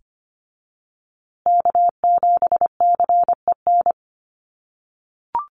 Morse code, heard as a keyed tone, about 25 wpm, 700 Hz. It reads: K7CEN
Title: USA Call Signs Repeated 3x 25wpm